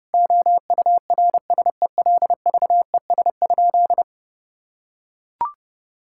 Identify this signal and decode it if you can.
Morse code, heard as a keyed tone, about 30 wpm, 700 Hz. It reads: OURSELVES?
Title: Sentences from Top 1600 Words 30wpm